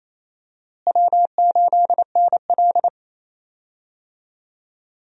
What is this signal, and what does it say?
Morse code, heard as a keyed tone, about 28 wpm, 700 Hz. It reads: W8NL
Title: USA Call Signs - 4 Characters - Repeated 3x - 28wpm